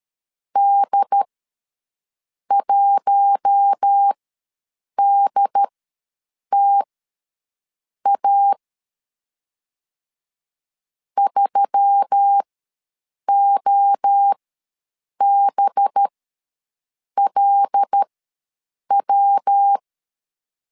Morse code, heard as a keyed tone, about 13 wpm, spaced as for 6 wpm, 800 Hz. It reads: D1DTA 3OBLW
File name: 5 LB ws 12wpm